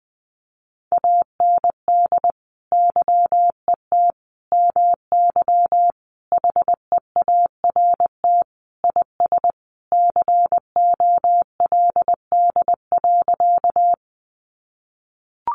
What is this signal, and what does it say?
Morse code, heard as a keyed tone, about 20 wpm, 700 Hz. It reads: AND YET MY HEART IS COLD.
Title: Sentences from Top 1200 Words 20wpm